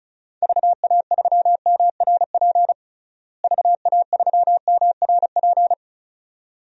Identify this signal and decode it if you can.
Morse code, heard as a keyed tone, about 34 wpm, 700 Hz. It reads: VA3MRP VA3MRP
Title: DXCC-100 Call Signs - 6 Characters - Repeated 2x - 35wpm